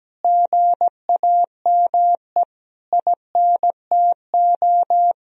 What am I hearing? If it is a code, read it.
Morse code, heard as a keyed tone, about 17 wpm, 700 Hz. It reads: GAME INTO
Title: Sentences from Top 1000 Words - Review Part 3 - 17wpm